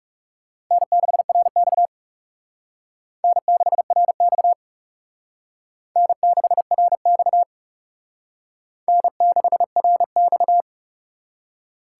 Morse code, heard as a keyed tone, about 39 wpm, 700 Hz. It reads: N6RX N6RX N6RX N6RX